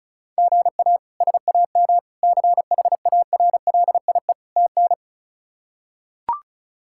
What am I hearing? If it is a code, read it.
Morse code, heard as a keyed tone, about 36 wpm, 700 Hz. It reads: GA SAM CHARLIE TN
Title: K1USN SST QSO Practice - 35wpm